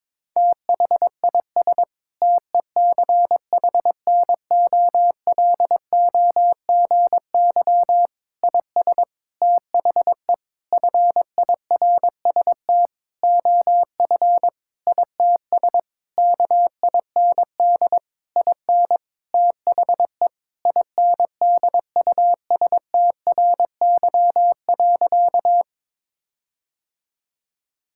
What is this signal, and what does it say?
Morse code, heard as a keyed tone, about 22 wpm, 700 Hz. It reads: THIS TECHNOLOGY IS THE FIRST OF ITS KIND IN THE INDUSTRY.
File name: practice / 22wpm / Sentences with 8-Word Phrases - 22wpm